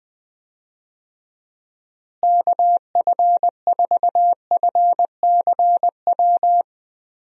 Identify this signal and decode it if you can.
Morse code, heard as a keyed tone, about 20 wpm, 700 Hz. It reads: KF4FCW
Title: USA Call Signs - 6 Characters - Repeated 3x - 20wpm